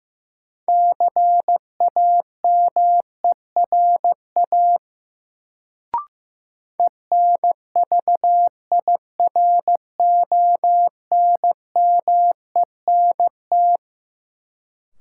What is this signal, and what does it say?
Morse code, heard as a keyed tone, about 15 wpm, 700 Hz. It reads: CAMERA ENVIRONMENT
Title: Sentences from Top 1000 Words - Review Part 5 - New Word Drill - 15wpm